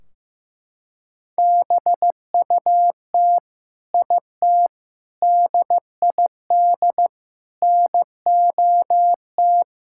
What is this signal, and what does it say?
Morse code, heard as a keyed tone, about 15 wpm, 700 Hz. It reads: BUT IT DID NOT
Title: Sentences from Top 1600 Words 15wpm